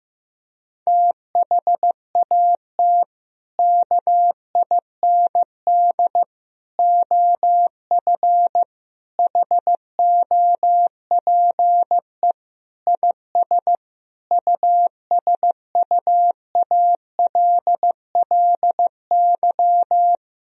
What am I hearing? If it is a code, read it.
Morse code, heard as a keyed tone, about 15 wpm, 700 Hz. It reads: THAT KIND OF HOPE IS USUALLY
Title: Sentences from Top 1000 Words - Review Part 3 - 15wpm